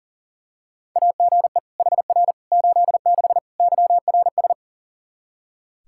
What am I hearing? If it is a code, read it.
Morse code, heard as a keyed tone, about 37 wpm, 700 Hz. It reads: AGE HR 86 YRS
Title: Practice QSO - Exchange 3 - More Info - 40wpm